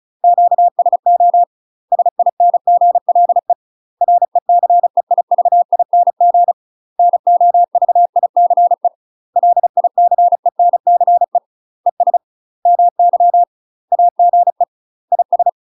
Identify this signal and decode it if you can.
Morse code, heard as a keyed tone, about 34 wpm, 700 Hz. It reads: QSO SINGLE RECEIVING NOVICE LICENCE ES MY AGE IS
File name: UK MASTER QSOs TRAINING / 35WPM / UK MASTER QSOs 0021